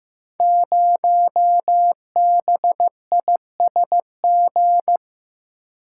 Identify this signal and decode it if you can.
Morse code, heard as a keyed tone, about 15 wpm, 700 Hz. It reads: 0BISG
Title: Code Groups - 5 Letter:Numbers - 15wpm